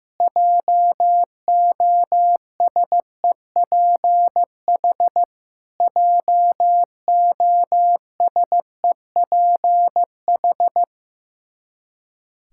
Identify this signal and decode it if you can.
Morse code, heard as a keyed tone, about 15 wpm, 700 Hz. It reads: JOSEPH JOSEPH
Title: Practice QSO - Exchange 1 - Beginning - 15wpm